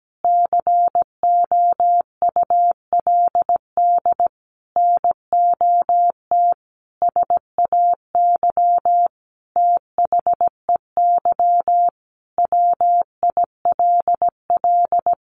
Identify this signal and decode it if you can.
Morse code, heard as a keyed tone, about 17 wpm, 700 Hz. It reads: COULD NOT SAY THEY WILL
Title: Sentences from Top 1100 Words 17wpm